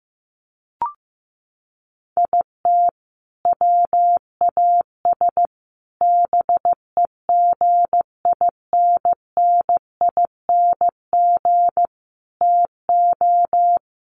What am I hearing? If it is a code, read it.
Morse code, heard as a keyed tone, about 15 wpm, 700 Hz. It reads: IT WAS BEGINNING TO
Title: Sentences from Top 800 Words 15wpm